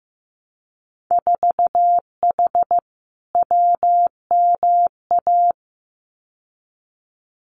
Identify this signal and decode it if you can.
Morse code, heard as a keyed tone, about 15 wpm, 700 Hz. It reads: 4H WMA
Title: Winter Field Day Exchanges Encore 15wpm